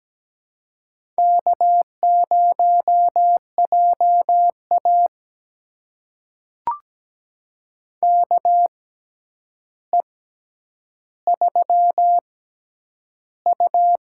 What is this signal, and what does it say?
Morse code, heard as a keyed tone, about 17 wpm, spaced as for 9 wpm, 700 Hz. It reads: K0JA KE3U
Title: USA Call Signs - 4 Characters - 6x Character Spacing - 17wpm